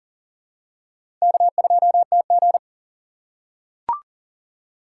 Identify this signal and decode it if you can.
Morse code, heard as a keyed tone, about 40 wpm, 700 Hz. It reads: K2TG
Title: USA Call Signs - 4x Character Spacing - 40wpm